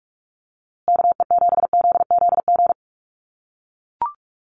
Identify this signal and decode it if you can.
Morse code, heard as a keyed tone, about 44 wpm, 700 Hz. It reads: KE7ZZG
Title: USA Call Signs Repeated 2x 45wpm